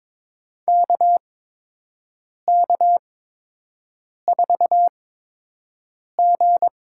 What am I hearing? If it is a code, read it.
Morse code, heard as a keyed tone, about 22 wpm, spaced as for 6 wpm, 700 Hz. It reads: KK4G
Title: USA Call Signs - 8x Character Spacing - 22wpm